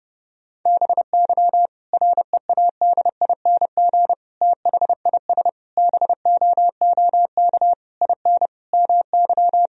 Morse code, heard as a keyed tone, about 30 wpm, 700 Hz. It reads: BY READING THIS BOOK IN MY
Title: Sentences from Top 1000 Words - Review Part 3 - 30wpm